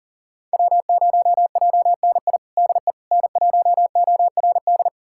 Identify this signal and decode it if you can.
Morse code, heard as a keyed tone, about 40 wpm, 700 Hz. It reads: W0JNI DE N1ORD
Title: USA Callsign DE Callsign - 5 Characters - 40wpm